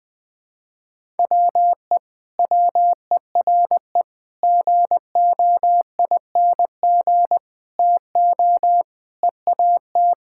Morse code, heard as a keyed tone, about 21 wpm, 700 Hz. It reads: WE WERE GOING TO EAT
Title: Sentences with 4-Word Phrases - 20wpm